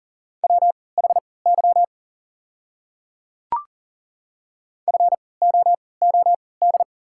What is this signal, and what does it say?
Morse code, heard as a keyed tone, about 40 wpm, spaced as for 23 wpm, 690 Hz. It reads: WHY FOOD
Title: 3x Character Spacing - Top 300 Words - 40wpm